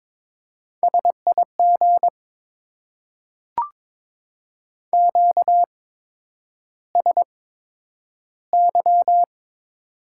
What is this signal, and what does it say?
Morse code, heard as a keyed tone, about 22 wpm, spaced as for 9 wpm, 700 Hz. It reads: SIG QSY
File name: Top 90 QSO Elements - 8x Character Spacing - 22wpm